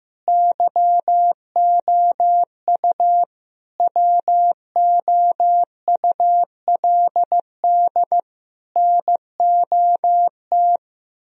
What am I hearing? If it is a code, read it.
Morse code, heard as a keyed tone, about 15 wpm, 700 Hz. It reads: YOU WOULD NOT
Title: Sentences from Top 1000 Words - Review Part 4 - 15wpm